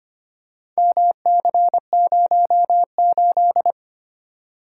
Morse code, heard as a keyed tone, about 25 wpm, 700 Hz. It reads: MC08
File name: practice / 25wpm / Four Letter-Number Combinations - 25wpm